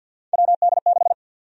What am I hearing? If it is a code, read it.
Morse code, heard as a keyed tone, about 52 wpm, 700 Hz. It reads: WD6
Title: USA Call Sign Prefixes - Repeated 2x - 50wpm